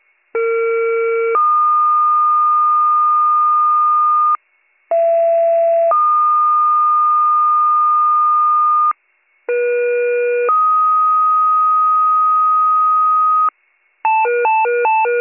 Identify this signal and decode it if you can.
Two-tone page, heard as a one-second tone, then a three-second tone, about 460 then 1220 Hz, about 670 then 1220 Hz, about 480 then 1250 Hz. It is heard from 0.3 to 13.5 s.